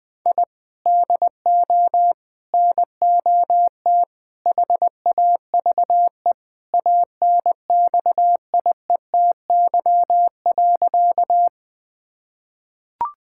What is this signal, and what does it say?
Morse code, heard as a keyed tone, about 20 wpm, 700 Hz. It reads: I DO NOT HAVE ANXIETY.
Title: Sentences from Top 1600 Words 20wpm